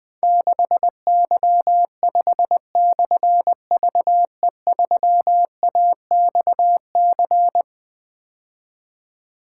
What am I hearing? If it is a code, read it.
Morse code, heard as a keyed tone, about 21 wpm, 700 Hz. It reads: 6Y5/VE3AXC
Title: Difficult Call Signs 20wpm